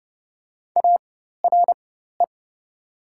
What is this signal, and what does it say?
Morse code, heard as a keyed tone, about 30 wpm, spaced as for 14 wpm, 700 Hz. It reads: ARE
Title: 4x Character Spacing - Top 100 Words - 30wpm